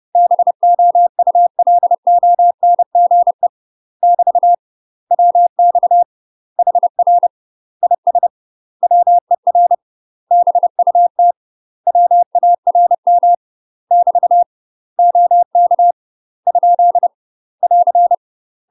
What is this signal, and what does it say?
Morse code, heard as a keyed tone, about 30 wpm, 700 Hz. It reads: BOULONGE <BT> WX HR IS WER BUT WARM <BT> OK ? <AR>
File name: UK MASTER QSOs TRAINING / 30WPM / UK MASTER QSOs 0036